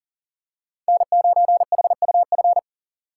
Morse code, heard as a keyed tone, about 40 wpm, 690 Hz. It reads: N9HUF
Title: USA Call Signs - 5 Characters - 3x Character Spacing - 40wpm